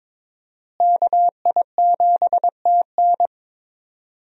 Morse code, heard as a keyed tone, about 22 wpm, 700 Hz. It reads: KI7TN
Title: USA Call Signs - 5 Characters - Repeated 2x - 22wpm